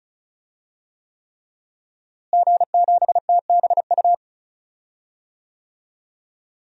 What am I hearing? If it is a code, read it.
Morse code, heard as a keyed tone, about 34 wpm, 700 Hz. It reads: G7TBU
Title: DXCC-100 Call Signs - 5 Characters - Repeated 3x - 35wpm